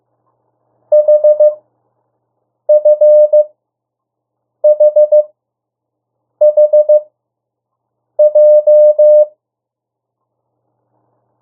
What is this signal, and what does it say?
Morse code, heard as a keyed tone, about 15 wpm, spaced as for 6 wpm, 600 Hz. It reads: HFHHJ
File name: morse mp3 / 02 urok 1